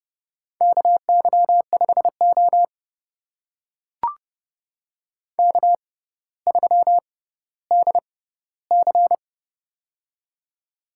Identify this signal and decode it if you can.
Morse code, heard as a keyed tone, about 30 wpm, spaced as for 15 wpm, 700 Hz. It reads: KY5O K3DC
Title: USA Call Signs - 4 Characters - 6x Character Spacing - 30wpm